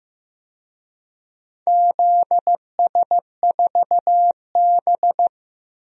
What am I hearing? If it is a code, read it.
Morse code, heard as a keyed tone, about 15 wpm, 700 Hz. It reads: ZS4B
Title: DXCC-100 Call Signs - 4 Characters - Repeated 2x - 15wpm